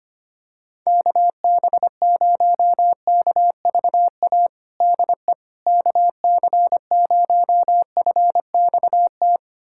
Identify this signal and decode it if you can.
Morse code, heard as a keyed tone, about 25 wpm, 700 Hz. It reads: KB0KVA DE KC0FXT
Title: USA Callsign DE Callsign - 6 Characters - 25wpm